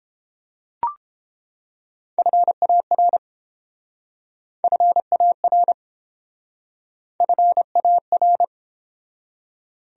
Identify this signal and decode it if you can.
Morse code, heard as a keyed tone, about 31 wpm, 700 Hz. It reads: FAR FAR FAR